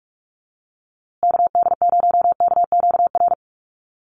Morse code, heard as a keyed tone, about 44 wpm, 700 Hz. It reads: KD0KQR
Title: USA Call Signs Repeated 2x 45wpm